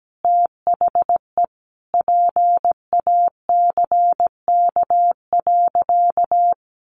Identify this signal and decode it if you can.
Morse code, heard as a keyed tone, about 17 wpm, 700 Hz. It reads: THE PACK.
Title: Sentences from Top 1300 Words 17wpm